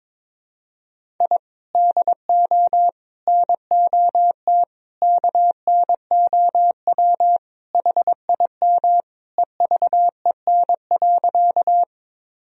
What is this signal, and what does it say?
Morse code, heard as a keyed tone, about 22 wpm, 700 Hz. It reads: I DO NOT KNOW HIM EVEN.
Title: Sentences from Top 1000 Words - Review Part 1 - 22wpm